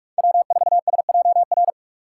Morse code, heard as a keyed tone, about 41 wpm, 700 Hz. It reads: W4SJR